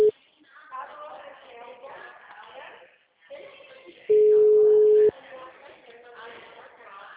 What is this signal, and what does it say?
Ringback tone (Europe, 425 Hz), with speech.